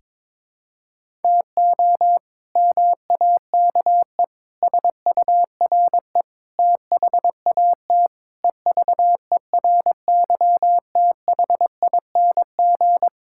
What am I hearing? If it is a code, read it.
Morse code, heard as a keyed tone, about 22 wpm, 700 Hz. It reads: TO MAKE SURE THAT EVERYTHING
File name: practice / 22wpm / Sentences with 4-Word Phrases - 22wpm